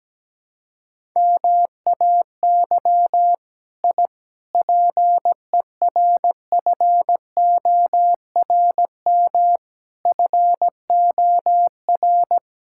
Morse code, heard as a keyed tone, about 17 wpm, 700 Hz. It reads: MAY I PERFORM FOR
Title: Sentences from Top 1000 Words - Review Part 5 - 17wpm